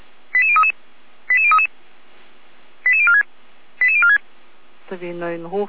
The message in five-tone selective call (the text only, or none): 88033 88033 88046 88046